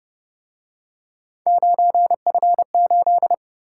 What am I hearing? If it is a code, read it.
Morse code, heard as a keyed tone, about 29 wpm, 700 Hz. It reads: 9F8
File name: Speed-Racing - Three Letter-Number Combinations - 25wpm